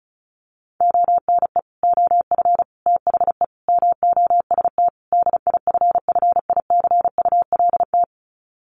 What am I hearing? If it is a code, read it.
Morse code, heard as a keyed tone, about 34 wpm, 700 Hz. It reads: ONE OF THE MOST DIFFICULT
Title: Sets of 5 Words Encore2 35wpm